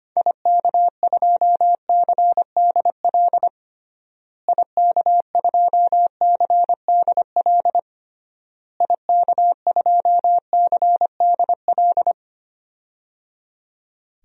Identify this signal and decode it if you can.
Morse code, heard as a keyed tone, about 25 wpm, 700 Hz. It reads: IK2CDL IK2CDL IK2CDL
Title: DXCC-100 Call Signs - 6 Characters - Repeated 3x - 25wpm